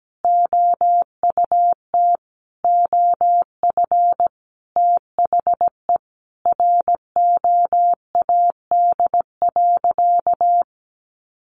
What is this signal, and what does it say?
Morse code, heard as a keyed tone, about 17 wpm, 700 Hz. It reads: OUT OF THE ROAD.
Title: Sentences from Top 400 Words 17wpm